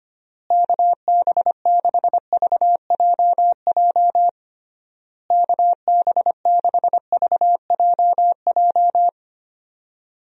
Morse code, heard as a keyed tone, about 25 wpm, 700 Hz. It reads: KB6VJJ KB6VJJ
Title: USA Call Signs - 6 Characters - Repeated 3x - 25wpm